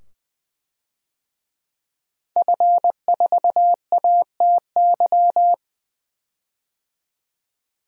Morse code, heard as a keyed tone, about 20 wpm, 700 Hz. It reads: F4ATY
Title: DXCC-100 Call Signs - 5 Characters - 20wpm